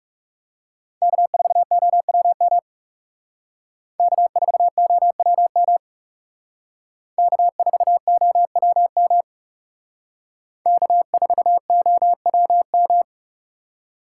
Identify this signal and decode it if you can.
Morse code, heard as a keyed tone, about 39 wpm, 700 Hz. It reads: K4OWM K4OWM K4OWM K4OWM